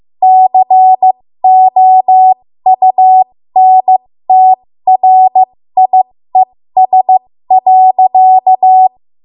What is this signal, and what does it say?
Morse code, heard as a keyed tone, about 15 wpm, spaced as for 13 wpm, 750 Hz. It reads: COUNTRIES.
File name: Morse / 13WPM